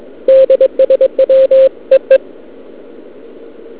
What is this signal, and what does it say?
Morse code, heard as a keyed tone, about 22 wpm, 530 Hz. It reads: DSW EE
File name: ARCHIV / KX3 / Scan 160m